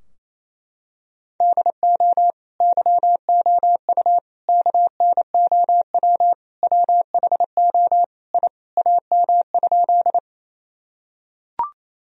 Morse code, heard as a keyed tone, about 28 wpm, 700 Hz. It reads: DO YOU KNOW WHO I AM?
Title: Sentences from Top 1000 Words - Review Part 2 - 28wpm